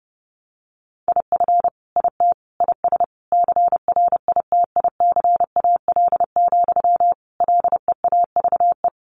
Morse code, heard as a keyed tone, about 28 wpm, 700 Hz. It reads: IF IT IS CRITICAL, LEAVE
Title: Sentences from Top 1100 Words 30wpm